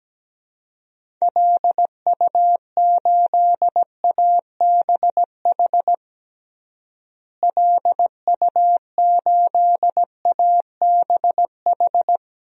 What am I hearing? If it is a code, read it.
Morse code, heard as a keyed tone, about 17 wpm, 700 Hz. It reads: LU8ABH LU8ABH
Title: DXCC-100 Call Signs - 6 Characters - Repeated 2x - 17wpm